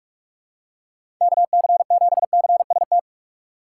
Morse code, heard as a keyed tone, about 42 wpm, 700 Hz. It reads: KC7CST
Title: USA Call Signs - 6 Characters - 3x Character Spacing - 45wpm